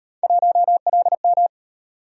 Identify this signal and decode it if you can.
Morse code, heard as a keyed tone, about 35 wpm, 700 Hz. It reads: 1LM